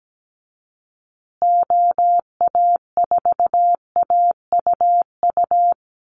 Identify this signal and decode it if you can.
Morse code, heard as a keyed tone, about 17 wpm, 700 Hz. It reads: OA4AUU
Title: USA Call Signs Repeated 2x 17wpm